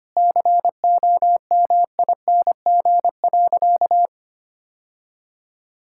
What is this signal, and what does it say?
Morse code, heard as a keyed tone, about 25 wpm, 700 Hz. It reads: COMING.
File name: practice / 25wpm / Sentences with 4-Word Phrases - 25wpm